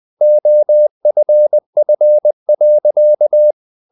Morse code, heard as a keyed tone, about 20 wpm, 600 Hz. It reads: OFF.
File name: the old man and the sea part 2 20wpm